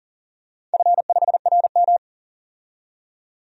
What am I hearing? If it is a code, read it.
Morse code, heard as a keyed tone, about 40 wpm, 700 Hz. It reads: F5RM
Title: DXCC-100 Call Signs - 4 Characters - 40wpm